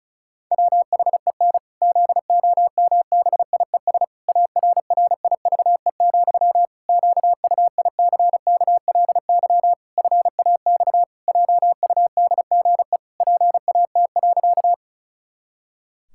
Morse code, heard as a keyed tone, about 36 wpm, 700 Hz. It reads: WHEN ZOMBIES ARRIVE, QUICKLY FAX JUDGE PAT.
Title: Pangram Challenge v2 - 35wpm